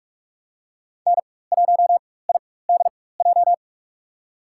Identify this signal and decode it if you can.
Morse code, heard as a keyed tone, about 47 wpm, spaced as for 21 wpm, 700 Hz. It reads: N1IDJ
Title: USA Call Signs - 5 Characters - 4x Character Spacing - 45wpm